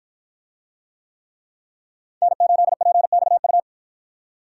Morse code, heard as a keyed tone, about 53 wpm, 700 Hz. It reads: N8PXU